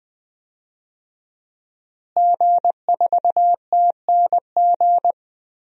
Morse code, heard as a keyed tone, about 21 wpm, 700 Hz. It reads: G4TNG
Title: DXCC-100 Call Signs - 5 Characters - Repeated 2x - 20wpm